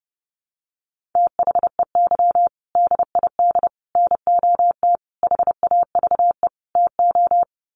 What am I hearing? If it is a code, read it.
Morse code, heard as a keyed tone, about 30 wpm, 700 Hz. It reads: THEY DID NOT HAVE TO
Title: Sets of 5 Words Encore2 30wpm